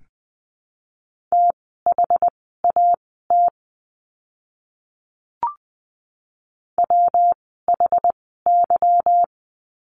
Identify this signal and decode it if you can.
Morse code, heard as a keyed tone, about 20 wpm, spaced as for 14 wpm, 700 Hz. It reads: THAT WHY
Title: 244 Lesson 39f - Letter z - Words 20wpm